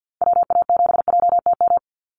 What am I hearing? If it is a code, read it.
Morse code, heard as a keyed tone, about 50 wpm, 700 Hz. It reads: WA7JTM
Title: NA SOTA Activator Callsigns 50wpm